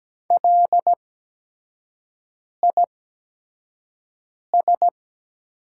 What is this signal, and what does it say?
Morse code, heard as a keyed tone, about 17 wpm, spaced as for 5 wpm, 700 Hz. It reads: LIS
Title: 8x Character Spacing - Top 500 Words - 17wpm